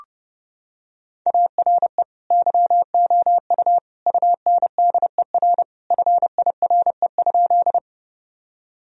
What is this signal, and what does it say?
Morse code, heard as a keyed tone, about 30 wpm, 700 Hz. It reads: ARE YOU UNDER FIRE?
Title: Sentences from Top 1000 Words - Review Part 1 - 30wpm